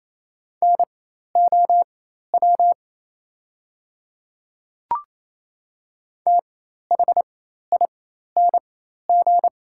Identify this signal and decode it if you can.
Morse code, heard as a keyed tone, about 27 wpm, spaced as for 13 wpm, 690 Hz. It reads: NOW THING
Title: 4x Character Spacing - Top 200 Words - 28wpm